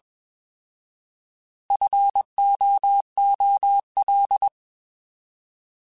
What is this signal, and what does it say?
Morse code, heard as a keyed tone, about 21 wpm, 800 Hz. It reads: FOOL
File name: Alphabetical / Alphabetical mp3s / Fs @20